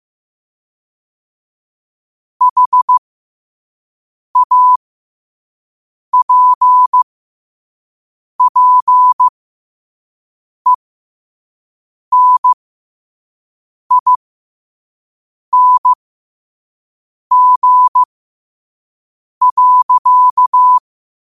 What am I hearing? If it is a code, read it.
Morse code, heard as a keyed tone, about 15 wpm, spaced as for 5 wpm, 1000 Hz. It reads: HAPPENING.
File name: qotdfiles / 5wpm